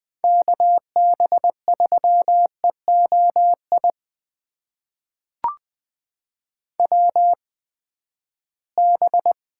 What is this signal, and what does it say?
Morse code, heard as a keyed tone, about 20 wpm, spaced as for 14 wpm, 700 Hz. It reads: KB3EOI WB
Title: USA Call Signs - 6 Characters - 8x Character Spacing - 20wpm